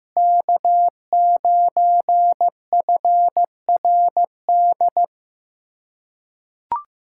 Morse code, heard as a keyed tone, about 15 wpm, 700 Hz. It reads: K9FRD
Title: USA Callsign DE Callsign 15wpm